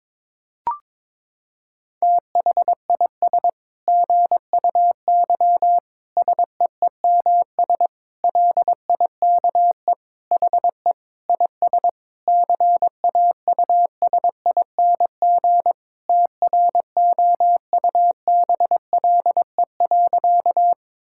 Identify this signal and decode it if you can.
Morse code, heard as a keyed tone, about 22 wpm, 700 Hz. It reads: THIS GUY SEEMS LIKE HE IS CAUSING TROUBLE.
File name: Sentences with 9-Word Phrases - 22wpm